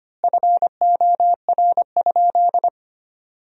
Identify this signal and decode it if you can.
Morse code, heard as a keyed tone, about 25 wpm, 700 Hz. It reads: FOR?
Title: 3x Word-Spacing - Sentences from Top 200 Words - 25wpm